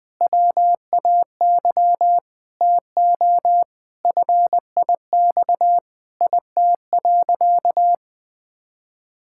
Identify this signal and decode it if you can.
Morse code, heard as a keyed tone, about 20 wpm, 700 Hz. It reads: WAY TO FIX IT.
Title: Sentences with 6-Word Phrases - 20wpm